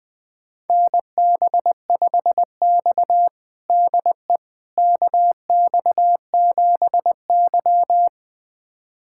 Morse code, heard as a keyed tone, about 20 wpm, 700 Hz. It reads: NB5X DE KX7Y
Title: USA Callsign DE Callsign - 4 Characters - 20wpm